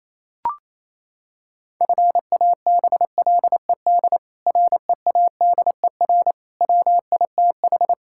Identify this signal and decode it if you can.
Morse code, heard as a keyed tone, about 28 wpm, 700 Hz. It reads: FABLED READER WITH
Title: Pangram Challenge v2 - 28wpm